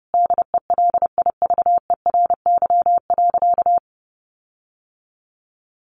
Morse code, heard as a keyed tone, about 30 wpm, 700 Hz. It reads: DELIVERY.
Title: Sentences from Top 1500 Words 30wpm